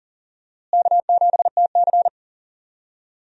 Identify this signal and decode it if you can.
Morse code, heard as a keyed tone, about 40 wpm, 700 Hz. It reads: K7TC
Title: USA Call Signs - 4 Characters - Repeated 2x - 40wpm